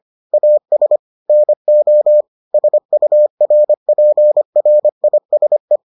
Morse code, heard as a keyed tone, about 25 wpm, 600 Hz. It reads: AS NO SURPRISE